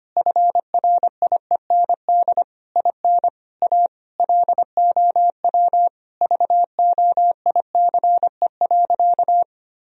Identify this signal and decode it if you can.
Morse code, heard as a keyed tone, about 25 wpm, 700 Hz. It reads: FRIEND IN A LOW VOICE.
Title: Sentences from Top 1000 Words - Review Part 1 - 25wpm